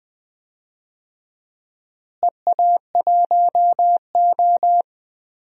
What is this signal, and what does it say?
Morse code, heard as a keyed tone, about 20 wpm, 700 Hz. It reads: EA1O
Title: DXCC-100 Call Signs - 4 Characters - Repeated 2x - 20wpm